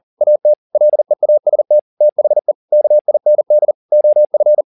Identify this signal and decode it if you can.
Morse code, heard as a keyed tone, about 40 wpm, 600 Hz. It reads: AT LEAST THE KIND OF
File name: the old man and the sea 2 40wpm